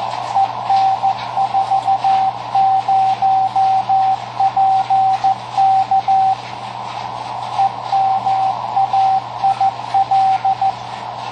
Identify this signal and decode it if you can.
Morse code, heard as a keyed tone, about 14 wpm, 770 Hz, with radio noise. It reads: EN40PK EMAIL
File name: tracks / KA9SZX 3W Macomb IL